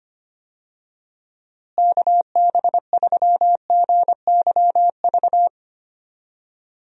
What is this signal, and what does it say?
Morse code, heard as a keyed tone, about 25 wpm, 700 Hz. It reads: KB3GYV
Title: USA Call Signs - 6 Characters - Repeated 2x - 25wpm